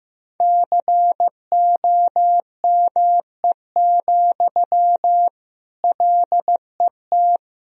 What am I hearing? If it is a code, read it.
Morse code, heard as a keyed tone, about 15 wpm, 700 Hz. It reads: COME, LET
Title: Sentences from Top 1000 Words - Review Part 4 - 15wpm